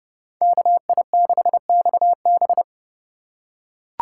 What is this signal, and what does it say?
Morse code, heard as a keyed tone, about 30 wpm, 700 Hz. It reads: KI6XB